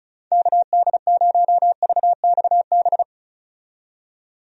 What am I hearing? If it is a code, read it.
Morse code, heard as a keyed tone, about 36 wpm, 700 Hz. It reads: KD0VXB
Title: USA Call Signs - 6 Characters - 35wpm